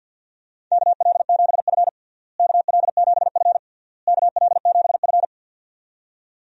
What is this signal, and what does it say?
Morse code, heard as a keyed tone, about 52 wpm, 700 Hz. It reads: KL7F KL7F KL7F
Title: DXCC-100 Call Signs - 4 Characters - Repeated 3x - 50wpm